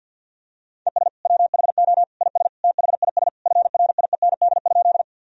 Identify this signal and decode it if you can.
Morse code, heard as a keyed tone, about 52 wpm, 700 Hz. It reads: ES WHO IS THIS FRIEND?
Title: Sentences from Top 300 Words and Abbreviations 50wpm